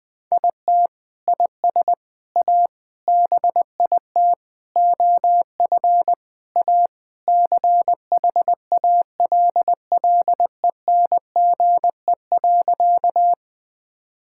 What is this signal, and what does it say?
Morse code, heard as a keyed tone, about 20 wpm, 700 Hz. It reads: IT IS A BIT OF A CHALLENGE.
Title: Sentences with 4-Word Phrases - Spoken Beginning - 20wpm